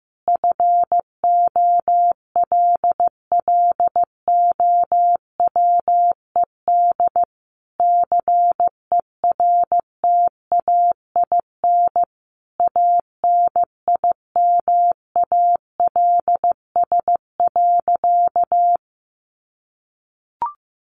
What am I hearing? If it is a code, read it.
Morse code, heard as a keyed tone, about 15 wpm, 700 Hz. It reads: FOLLOWED CERTAIN ANIMALS.
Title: A Sound Of Thunder 15wpm